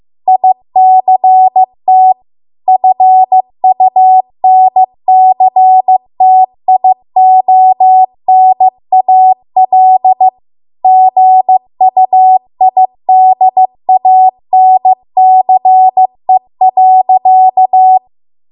Morse code, heard as a keyed tone, about 15 wpm, 750 Hz. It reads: ICT FUNCTIONAL GUIDANCE.